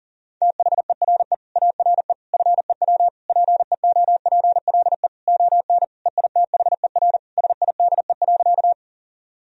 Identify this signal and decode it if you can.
Morse code, heard as a keyed tone, about 37 wpm, 700 Hz. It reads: THERE ARE FEW PEOPLE ON EITHER SIDE.
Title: Sentences from Top 1000 Words - Review Part 4 - 40wpm